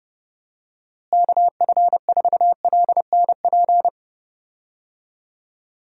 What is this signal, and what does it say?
Morse code, heard as a keyed tone, about 30 wpm, 700 Hz. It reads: KF4LNP
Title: USA Call Signs - 6 Characters - Repeated 3x - 30wpm